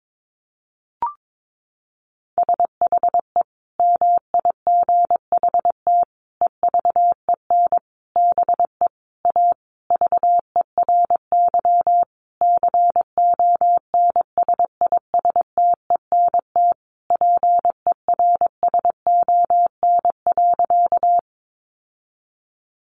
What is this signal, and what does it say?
Morse code, heard as a keyed tone, about 22 wpm, 700 Hz. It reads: SHE MIGHT EVEN BE A VERY CONSISTENT PERSON.
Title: Sentences from Top 1500 Words 22wpm